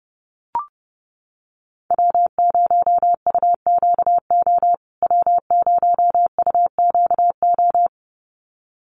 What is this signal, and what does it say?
Morse code, heard as a keyed tone, about 30 wpm, 700 Hz. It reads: W0UQO W0UQO
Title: USA Call Signs Repeated 2x 30wpm